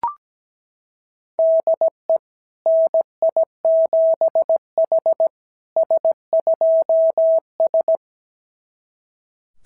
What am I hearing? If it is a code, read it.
Morse code, heard as a keyed tone, about 17 wpm, 660 Hz. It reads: DE NI7H S2S
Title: SOTA QSO Practice 17wpm